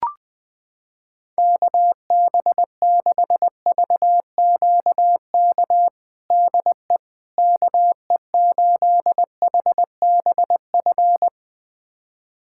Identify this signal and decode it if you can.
Morse code, heard as a keyed tone, about 20 wpm, 700 Hz. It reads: KB6VQK DE KE8HBF
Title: USA Callsign DE Callsign - 6 Characters - 20wpm